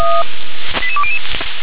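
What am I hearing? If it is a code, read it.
Five-tone selective call (ZVEI), heard as quick runs of five tones, tones of about 70 ms. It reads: 88200
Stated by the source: ZVEI five-tone selective call received with an SDR.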